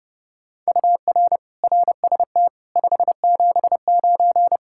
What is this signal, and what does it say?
Morse code, heard as a keyed tone, about 29 wpm, 700 Hz. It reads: UR RST 579